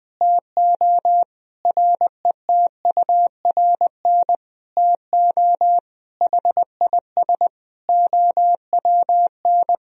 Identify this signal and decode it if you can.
Morse code, heard as a keyed tone, about 20 wpm, 700 Hz. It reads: TO RETURN TO HIS OWN
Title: Sentences from Top 1000 Words - Review Part 1 - 20wpm